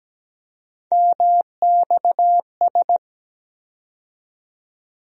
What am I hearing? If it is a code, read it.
Morse code, heard as a keyed tone, about 17 wpm, 700 Hz. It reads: MXS
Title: USA Call Sign Postfixes - 3 Letters 17wpm